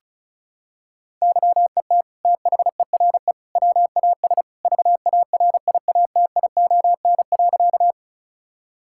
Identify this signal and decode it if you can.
Morse code, heard as a keyed tone, about 36 wpm, 700 Hz. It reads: YET THERE WAS VARIATION.
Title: Sentences from Top 1600 Words 35wpm